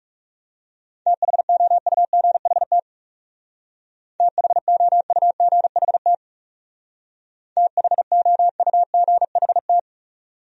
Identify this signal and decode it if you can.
Morse code, heard as a keyed tone, about 39 wpm, 700 Hz. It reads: THOUGHT THOUGHT THOUGHT